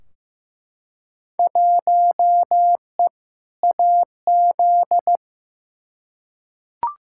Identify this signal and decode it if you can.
Morse code, heard as a keyed tone, about 15 wpm, 700 Hz. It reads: 1E AZ
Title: Field Day Exchanges - 15wpm